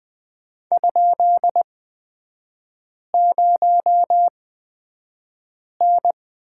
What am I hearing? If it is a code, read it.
Morse code, heard as a keyed tone, about 20 wpm, spaced as for 5 wpm, 700 Hz. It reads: ?0N
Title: 243 Lesson 39e - Letter z - Three Characters 20wpm